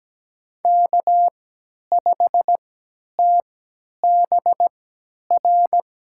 Morse code, heard as a keyed tone, about 17 wpm, spaced as for 10 wpm, 700 Hz. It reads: K5TBR
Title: USA Call Signs - 5 Characters - 3x Character Spacing - 17wpm